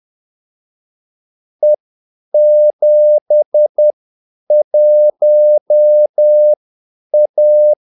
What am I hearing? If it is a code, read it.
Morse code, heard as a keyed tone, about 10 wpm, spaced as for 8 wpm, 600 Hz. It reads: E71A
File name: Callsigns-10-8wpm